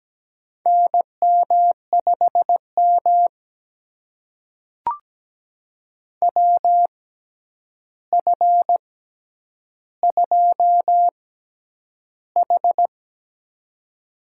Morse code, heard as a keyed tone, about 17 wpm, spaced as for 9 wpm, 700 Hz. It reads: NM5M WF2H
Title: USA Call Signs - 4 Characters - 6x Character Spacing - 17wpm